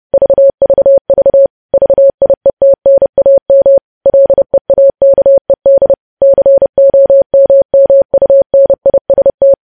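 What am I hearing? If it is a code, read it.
Morse code, heard as a keyed tone, about 30 wpm, 550 Hz. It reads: VVV VIETNAM LEAKED COMMUNIST
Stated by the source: Morse news headlines at 30 wpm.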